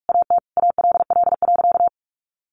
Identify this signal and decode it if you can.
Morse code, heard as a keyed tone, about 44 wpm, 700 Hz. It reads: AT ALL.
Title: A Sound Of Thunder 45wpm